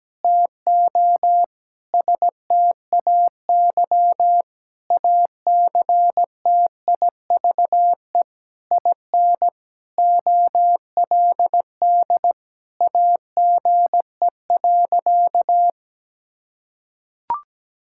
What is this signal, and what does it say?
Morse code, heard as a keyed tone, about 17 wpm, 700 Hz. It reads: TO STAY ACTIVE IN OLD AGE.
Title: Sentences from Top 1000 Words - Review Part 6 - 17wpm